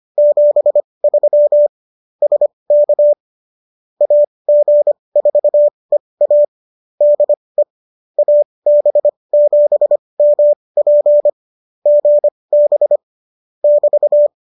Morse code, heard as a keyed tone, about 25 wpm, spaced as for 20 wpm, 570 Hz. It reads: 73 SK AG4EA DE AB7MP GB <BT>